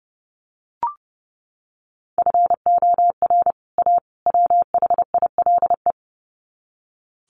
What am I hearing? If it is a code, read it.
Morse code, heard as a keyed tone, about 30 wpm, 700 Hz. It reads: FOR A WHILE
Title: Sets of 3 Words 30wpm